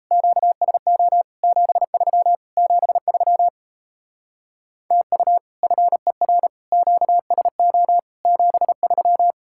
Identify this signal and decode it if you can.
Morse code, heard as a keyed tone, about 35 wpm, 700 Hz. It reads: QSO 73 73 TU FER QSO 73
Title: Speed-Racing Common QSO Phrases - 28wpm